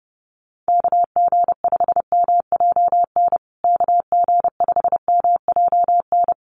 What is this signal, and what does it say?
Morse code, heard as a keyed tone, about 30 wpm, 700 Hz. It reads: KG5MJN KG5MJN
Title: USA Call Signs Repeated 2x 30wpm